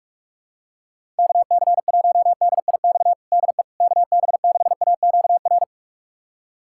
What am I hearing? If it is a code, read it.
Morse code, heard as a keyed tone, about 46 wpm, 700 Hz. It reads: KC1DIX DE KB6AQR
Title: USA Callsign DE Callsign 45wpm